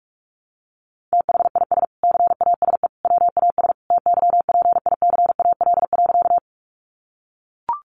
Morse code, heard as a keyed tone, about 44 wpm, 700 Hz. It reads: THIS CASE WAS TYPICAL.
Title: Sentences from Top 1100 Words 45wpm